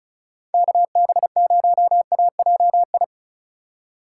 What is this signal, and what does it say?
Morse code, heard as a keyed tone, about 36 wpm, 700 Hz. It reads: KB0AJI
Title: USA Call Signs - 4x Character Spacing - 35wpm